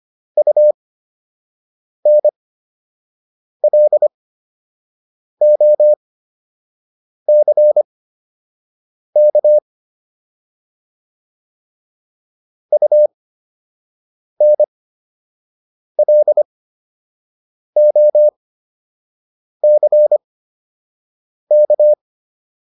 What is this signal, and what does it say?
Morse code, heard as a keyed tone, about 25 wpm, spaced as for 6 wpm, 600 Hz. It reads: UNLOCK UNLOCK